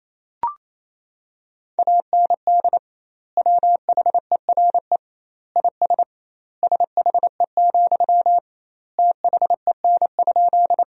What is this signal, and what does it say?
Morse code, heard as a keyed tone, about 28 wpm, 700 Hz. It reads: AND WHERE IS SHE, THEN?
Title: 2x Word-Spacing - Sentences from Top 200 Words - 28wpm